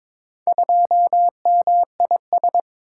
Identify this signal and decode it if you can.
Morse code, heard as a keyed tone, about 22 wpm, 700 Hz. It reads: 2MIS
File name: practice / 22wpm / Four Letter-Number Combinations - 22wpm